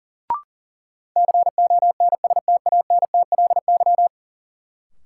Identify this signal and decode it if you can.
Morse code, heard as a keyed tone, about 40 wpm, 700 Hz. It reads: CONSTANTLY
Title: Sentences from Top 1000 Words - Review Part 7 - New Word Drill - 40wpm